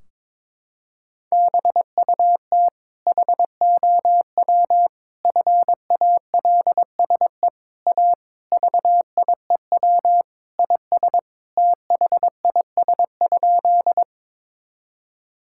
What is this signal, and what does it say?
Morse code, heard as a keyed tone, about 22 wpm, 700 Hz. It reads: BUT HOW FALSE A VIEW IS THIS?
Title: Sentences from Top 1600 Words 22wpm